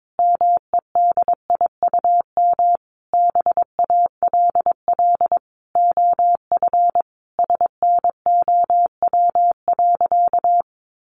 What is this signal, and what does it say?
Morse code, heard as a keyed tone, about 22 wpm, 700 Hz. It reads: MEDIUM BALL OF SNOW.
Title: Easy Phrases 22wpm